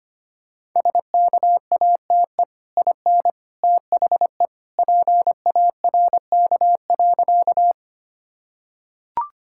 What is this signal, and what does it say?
Morse code, heard as a keyed tone, about 25 wpm, 700 Hz. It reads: SKATE IN THE PARK.
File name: Easy Sentences with Hints - 25wpm